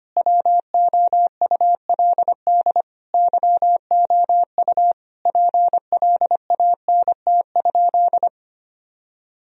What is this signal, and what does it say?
Morse code, heard as a keyed tone, about 25 wpm, 700 Hz. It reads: WOULD YOU PLANT?
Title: Sentences from Top 1000 Words - Review Part 1 - 25wpm